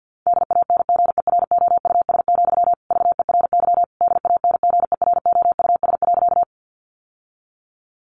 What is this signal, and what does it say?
Morse code, heard as a keyed tone, about 50 wpm, 700 Hz. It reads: DANGEROUS, VERY DANGEROUS.
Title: Sentences from Top 1200 Words 50wpm